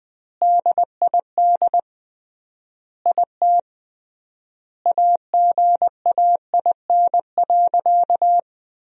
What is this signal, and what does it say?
Morse code, heard as a keyed tone, about 20 wpm, 700 Hz. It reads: DID IT AGAIN.
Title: 3x Word-Spacing - Sentences from Top 300 Words - 20wpm